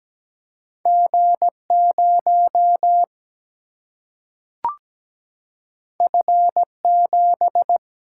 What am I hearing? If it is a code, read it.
Morse code, heard as a keyed tone, about 17 wpm, 700 Hz. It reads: G0 F7
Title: Single-Letter Single-Number 17wpm